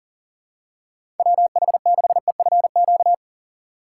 Morse code, heard as a keyed tone, about 37 wpm, 700 Hz. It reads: WH6EFQ
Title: USA Call Signs - 6 Characters - 4x Character Spacing - 40wpm